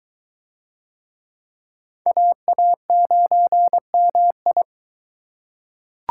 Morse code, heard as a keyed tone, about 23 wpm, 700 Hz. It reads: AA9MI